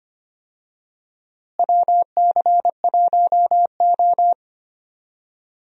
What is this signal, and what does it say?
Morse code, heard as a keyed tone, about 25 wpm, 700 Hz. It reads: WC1O
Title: USA Call Signs - 4 Characters - Repeated 2x - 25wpm